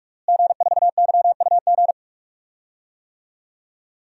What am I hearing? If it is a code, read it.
Morse code, heard as a keyed tone, about 43 wpm, 700 Hz. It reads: G4YUG
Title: DXCC-100 Call Signs - 5 Characters - Repeated 3x - 45wpm